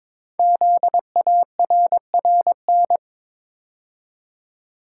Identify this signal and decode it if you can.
Morse code, heard as a keyed tone, about 23 wpm, 700 Hz. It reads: ZARRN
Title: Code Groups - 5 Letters - 22wpm